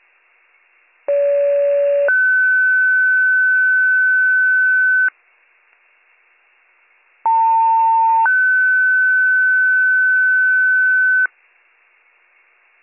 Two-tone page, heard as a one-second tone, then a three-second tone, about 570 then 1500 Hz, about 900 then 1500 Hz. It is heard from 1.1 to 11.3 s.